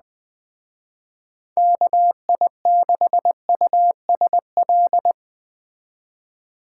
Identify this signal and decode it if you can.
Morse code, heard as a keyed tone, about 20 wpm, 700 Hz. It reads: KI6USL
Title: Speed Racing - USA 2x3 Call Signs 17wpm